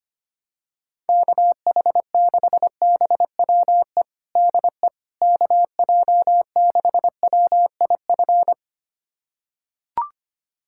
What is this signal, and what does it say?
Morse code, heard as a keyed tone, about 25 wpm, 700 Hz. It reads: KH6BWE DE KJ6WIF
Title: USA Callsign DE Callsign 25wpm